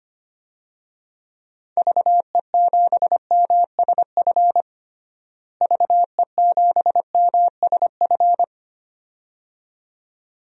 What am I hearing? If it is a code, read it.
Morse code, heard as a keyed tone, about 25 wpm, 700 Hz. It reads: VE7MSF VE7MSF
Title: DXCC-100 Call Signs - 6 Characters - Repeated 2x - 25wpm